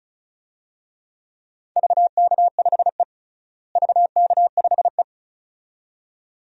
Morse code, heard as a keyed tone, about 36 wpm, 700 Hz. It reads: VK5E VK5E
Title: DXCC-100 Call Signs - 4 Characters - Repeated 2x - 35wpm